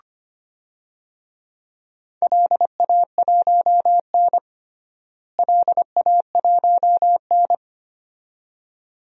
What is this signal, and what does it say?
Morse code, heard as a keyed tone, about 25 wpm, 700 Hz. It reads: LA1N LA1N
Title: DXCC-100 Call Signs - 4 Characters - Repeated 2x - 25wpm